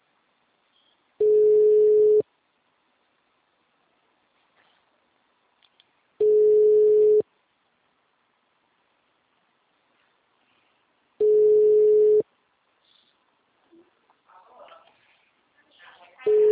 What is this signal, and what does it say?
Ringback tone (Europe, 425 Hz).